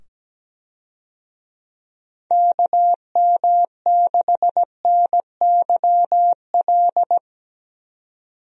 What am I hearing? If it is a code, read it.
Morse code, heard as a keyed tone, about 17 wpm, 700 Hz. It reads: KM6NYL
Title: USA Call Signs - 6 Characters - 17wpm